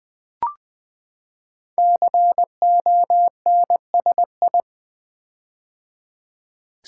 Morse code, heard as a keyed tone, about 20 wpm, 700 Hz. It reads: CONSI
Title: Build A Word - 5 Letters At A Time - 12-Letter Words - 20wpm